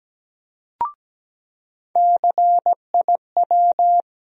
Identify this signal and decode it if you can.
Morse code, heard as a keyed tone, about 17 wpm, 700 Hz. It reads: CIW
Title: USA Call Sign Postfixes - 3 Letters 17wpm